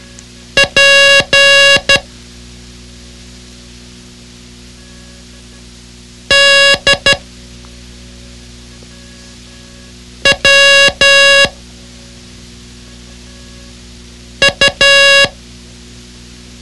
Morse code, heard as a keyed tone, about 12 wpm, 2800 Hz. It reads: P D W U
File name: ham radio / CodeQuick / Code Quick 06